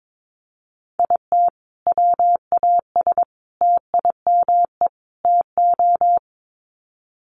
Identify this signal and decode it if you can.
Morse code, heard as a keyed tone, about 22 wpm, 700 Hz. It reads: IT WAS TIME TO
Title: Sets of 4 Words 22wpm